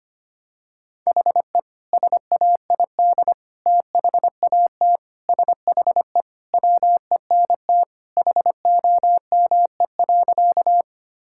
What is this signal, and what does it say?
Morse code, heard as a keyed tone, about 25 wpm, 700 Hz. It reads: HE SAID THAT SHE WENT HOME.
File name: practice / 25wpm / Sentences with 2-Word Phrases - 25wpm